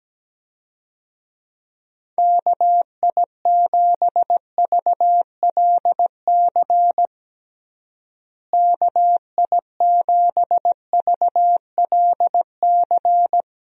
Morse code, heard as a keyed tone, about 17 wpm, 700 Hz. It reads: KI7VLC KI7VLC
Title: USA Call Signs - 6 Characters - Repeated 3x - 17wpm